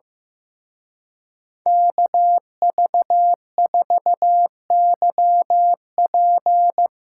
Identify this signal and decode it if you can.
Morse code, heard as a keyed tone, about 16 wpm, 700 Hz. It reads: KV4YP
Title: Speed Racing - USA Call Signs 15wpm